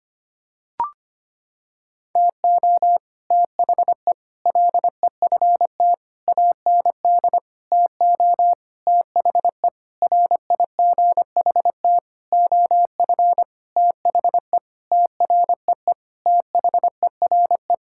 Morse code, heard as a keyed tone, about 25 wpm, 700 Hz. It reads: TO THE LEFT AND TO THE RIGHT OF THE TREE THERE
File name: practice / 25wpm / Sentences with 9-Word Phrases - 25wpm